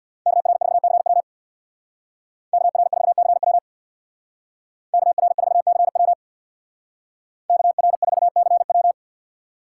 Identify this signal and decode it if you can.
Morse code, heard as a keyed tone, about 64 wpm, 700 Hz. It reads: KR4CW KR4CW KR4CW KR4CW